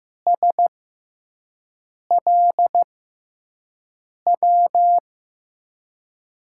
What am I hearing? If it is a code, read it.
Morse code, heard as a keyed tone, about 15 wpm, spaced as for 5 wpm, 700 Hz. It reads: SLW
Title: USA Call Signs - 5 Characters - 6x Character Spacing - 15wpm